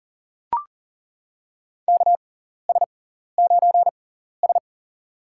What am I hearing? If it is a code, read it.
Morse code, heard as a keyed tone, about 40 wpm, spaced as for 14 wpm, 700 Hz. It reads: KS9S
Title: USA Call Signs - 4 Characters - 6x Character Spacing - 40wpm